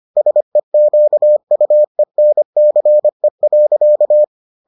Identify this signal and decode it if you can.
Morse code, heard as a keyed tone, about 25 wpm, 600 Hz. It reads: SEQUENCE.